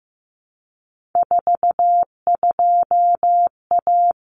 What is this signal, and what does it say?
Morse code, heard as a keyed tone, about 15 wpm, 700 Hz. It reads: 42A
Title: Common Cut Numbers - 3-Digit 15wpm